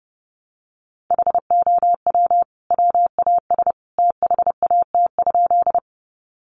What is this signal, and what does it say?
Morse code, heard as a keyed tone, about 30 wpm, 700 Hz. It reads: HOW WAS THAT?
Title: Sentences from Top 100 Words 30wpm